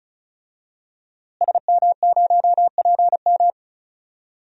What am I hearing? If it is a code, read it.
Morse code, heard as a keyed tone, about 36 wpm, 700 Hz. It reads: SM0PM
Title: DXCC-100 Call Signs - 5 Characters - Repeated 2x - 35wpm